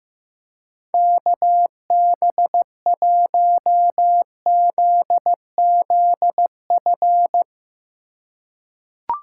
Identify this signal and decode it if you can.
Morse code, heard as a keyed tone, about 15 wpm, 700 Hz. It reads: KB1ZZF
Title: USA Call Signs - 3x Character Spacing - 15wpm